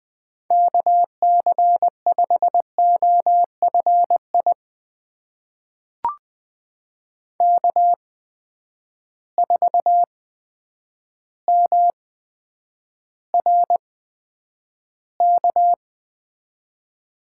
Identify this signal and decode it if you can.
Morse code, heard as a keyed tone, about 20 wpm, spaced as for 9 wpm, 700 Hz. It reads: KC5OFI K4MRK
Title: USA Call Signs - 8x Character Spacing - 20wpm